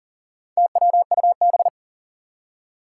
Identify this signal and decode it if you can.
Morse code, heard as a keyed tone, about 41 wpm, 700 Hz. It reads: TWUB
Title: Four Challenging Letter-Number Combinations 40wpm